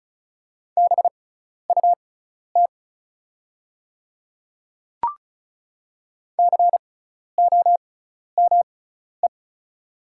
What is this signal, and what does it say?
Morse code, heard as a keyed tone, about 36 wpm, spaced as for 12 wpm, 700 Hz. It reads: BUT COME
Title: 6x Character Spacing - Top 100 Words - 35wpm